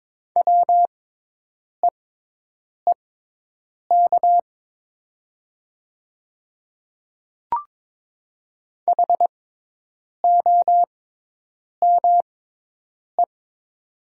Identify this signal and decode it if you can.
Morse code, heard as a keyed tone, about 22 wpm, spaced as for 8 wpm, 700 Hz. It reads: WEEK HOME
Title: 6x Character Spacing - Top 500 Words - 22wpm